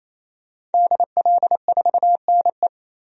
Morse code, heard as a keyed tone, about 28 wpm, 700 Hz. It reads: DL4NE
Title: DXCC-100 Call Signs - 5 Characters - 28wpm